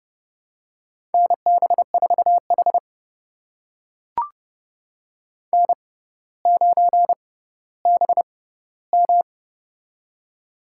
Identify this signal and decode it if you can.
Morse code, heard as a keyed tone, about 30 wpm, spaced as for 15 wpm, 700 Hz. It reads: NB4H N9BM
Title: USA Call Signs - 4 Characters - 6x Character Spacing - 30wpm